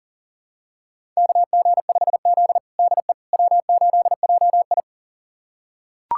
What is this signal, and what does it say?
Morse code, heard as a keyed tone, about 40 wpm, 700 Hz. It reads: KG5Z DE W8JI
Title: USA Callsign DE Callsign - 4 Characters - 40wpm